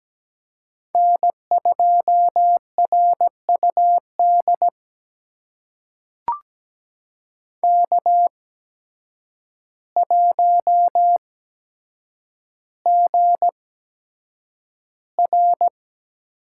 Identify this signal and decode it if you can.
Morse code, heard as a keyed tone, about 17 wpm, spaced as for 8 wpm, 700 Hz. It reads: N2RUD K1GR
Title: USA Call Signs - 8x Character Spacing - 17wpm